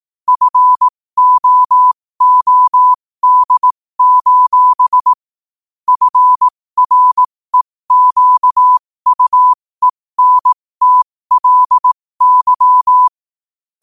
Morse code, heard as a keyed tone, about 18 wpm, spaced as for 15 wpm, 1000 Hz. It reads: FOOD: FREQUENTLY
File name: qotdfiles / 15wpm